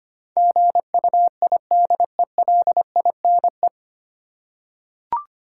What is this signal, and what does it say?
Morse code, heard as a keyed tone, about 25 wpm, 700 Hz. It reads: GUIDELINE
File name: Nine Letter Words - 25wpm